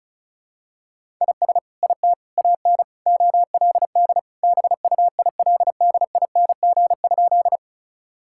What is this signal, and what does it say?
Morse code, heard as a keyed tone, about 36 wpm, 700 Hz. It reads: IS IT AN OLD BUILDING?
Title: Sentences from Top 1000 Words - Review Part 5 - 35wpm